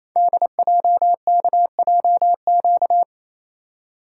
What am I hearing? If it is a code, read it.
Morse code, heard as a keyed tone, about 27 wpm, 700 Hz. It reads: DJKJQ
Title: Code Groups - 5 Letters - 28wpm